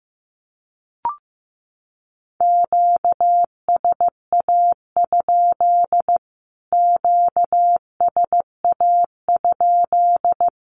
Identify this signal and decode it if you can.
Morse code, heard as a keyed tone, about 15 wpm, 700 Hz. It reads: QSA? QSA?
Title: Q-Codes Repeated 3x 15wpm